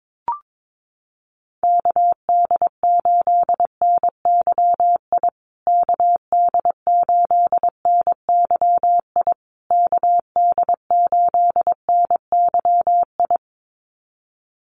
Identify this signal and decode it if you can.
Morse code, heard as a keyed tone, about 22 wpm, 700 Hz. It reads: KD8NYI KD8NYI KD8NYI
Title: USA Call Signs Repeated 3x 22wpm